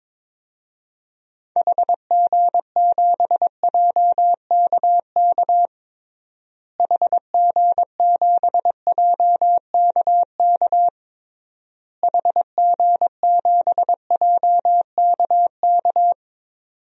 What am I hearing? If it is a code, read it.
Morse code, heard as a keyed tone, about 22 wpm, 700 Hz. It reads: HG7JKK HG7JKK HG7JKK
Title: DXCC-100 Call Signs - 6 Characters - Repeated 3x - 22wpm